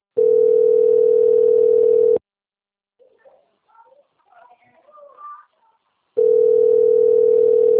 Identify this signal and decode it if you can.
Ringback tone (North America), 440 + 480 Hz.